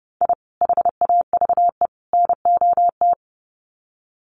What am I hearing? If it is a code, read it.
Morse code, heard as a keyed tone, about 29 wpm, 700 Hz. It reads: I HAVE NOT
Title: Sets of 3 Words 30wpm